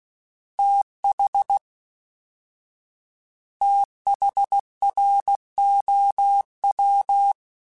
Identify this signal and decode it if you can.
Morse code, heard as a keyed tone, about 16 wpm, 800 Hz. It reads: TH THROW
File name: TH / TH mp3s / TH Prefixes 1x1 @13